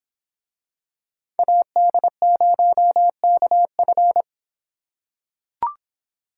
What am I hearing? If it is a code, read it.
Morse code, heard as a keyed tone, about 23 wpm, 700 Hz. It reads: AD0KF